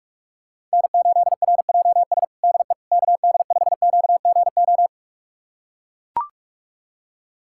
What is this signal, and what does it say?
Morse code, heard as a keyed tone, about 42 wpm, 700 Hz. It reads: N8RJS DE KD5QGO
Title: USA Callsign DE Callsign 45wpm